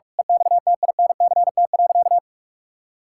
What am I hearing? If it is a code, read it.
Morse code, heard as a keyed tone, about 46 wpm, 700 Hz. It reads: EXTINCT.
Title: Pangram Challenge v2 - 45wpm